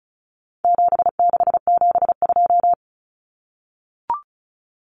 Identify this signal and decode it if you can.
Morse code, heard as a keyed tone, about 34 wpm, 700 Hz. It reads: 7672
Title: Common Cut Numbers - 4-Digit 35wpm